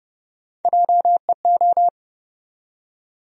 Morse code, heard as a keyed tone, about 30 wpm, 700 Hz. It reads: JEO
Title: USA Call Sign Postfixes - 3 Letters 30wpm